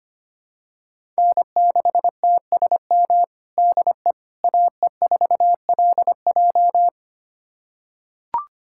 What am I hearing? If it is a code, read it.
Morse code, heard as a keyed tone, about 25 wpm, 700 Hz. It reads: N6TSM DE AE4LJ
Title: USA Callsign DE Callsign - 5 Characters - 25wpm